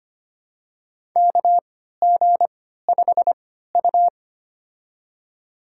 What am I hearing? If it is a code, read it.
Morse code, heard as a keyed tone, about 25 wpm, spaced as for 14 wpm, 700 Hz. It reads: KG5U
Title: USA Call Signs - 4 Characters - 3x Character Spacing - 25wpm